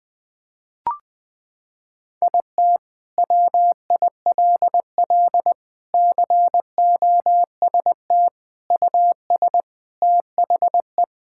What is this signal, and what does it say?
Morse code, heard as a keyed tone, about 20 wpm, 700 Hz. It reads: IT WILL COST US THE
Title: Sentences from Top 1000 Words - Review Part 5 - 20wpm